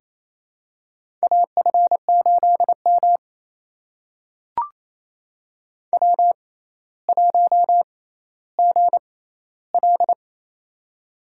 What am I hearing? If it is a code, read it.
Morse code, heard as a keyed tone, about 28 wpm, spaced as for 14 wpm, 700 Hz. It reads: AF8M W1GL
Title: USA Call Signs - 4 Characters - 6x Character Spacing - 28wpm